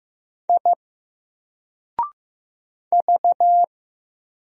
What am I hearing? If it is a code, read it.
Morse code, heard as a keyed tone, about 15 wpm, 700 Hz. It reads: I V
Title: Single Character (Letter:Number:Punctuation:Prosign) 15wpm